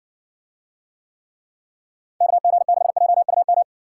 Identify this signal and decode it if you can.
Morse code, heard as a keyed tone, about 57 wpm, 700 Hz. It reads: KG6JUM